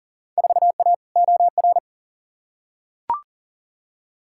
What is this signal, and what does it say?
Morse code, heard as a keyed tone, about 40 wpm, 700 Hz. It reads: 4A OR
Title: Field Day Exchanges - 40wpm